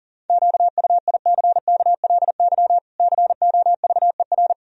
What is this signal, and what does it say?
Morse code, heard as a keyed tone, about 40 wpm, 700 Hz. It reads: QUICKLY COVER
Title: Sentences from Top 1000 Words - Review Part 1 - 40wpm